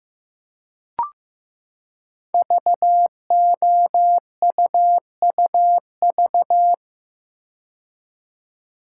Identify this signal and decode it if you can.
Morse code, heard as a keyed tone, about 15 wpm, 700 Hz. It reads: VOUUV
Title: Code Groups - 5 Letters - 15wpm